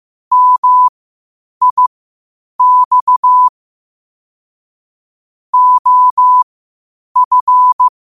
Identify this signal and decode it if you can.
Morse code, heard as a keyed tone, about 15 wpm, spaced as for 8 wpm, 1000 Hz. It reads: MIX OF
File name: qotdfiles / 8wpm